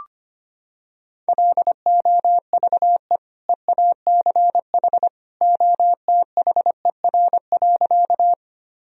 Morse code, heard as a keyed tone, about 25 wpm, 700 Hz. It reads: LOVE EACH OTHER.
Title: Sentences from Top 1000 Words - Review Part 1 - 25wpm